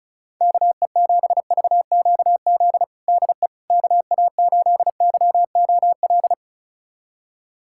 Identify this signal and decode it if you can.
Morse code, heard as a keyed tone, about 36 wpm, 700 Hz. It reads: KE7VQZ DE KA8YOL
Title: USA Callsign DE Callsign - 6 Characters - 35wpm